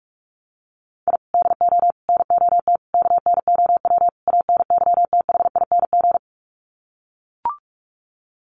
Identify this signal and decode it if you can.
Morse code, heard as a keyed tone, about 44 wpm, 700 Hz. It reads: I DO NOT KNOW ANYTHING
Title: Sets of 5 Words Encore 45wpm